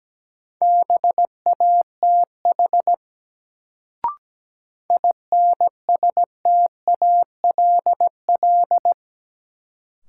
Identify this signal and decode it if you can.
Morse code, heard as a keyed tone, about 17 wpm, 700 Hz. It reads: BATH INSTALL
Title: Sentences from Top 1600 Words - New Word Drill Alt - 17wpm